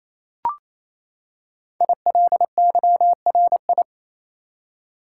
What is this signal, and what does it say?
Morse code, heard as a keyed tone, about 27 wpm, 700 Hz. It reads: ILYRI
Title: Code Groups - 5 Letters - 28wpm